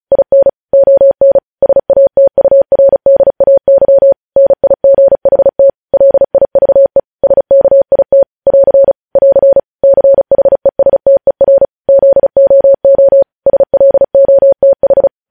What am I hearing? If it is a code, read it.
Morse code, heard as a keyed tone, about 36 wpm, 550 Hz. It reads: IN ON SATURDAY NIGHT LIVE SKIT <AR> <AR> CHESTER ZOO SLOTH
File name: podcast / News headlines-1x-35x35